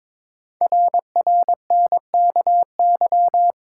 Morse code, heard as a keyed tone, about 23 wpm, 700 Hz. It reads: RRNKY
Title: Code Groups - 5 Letters - 22wpm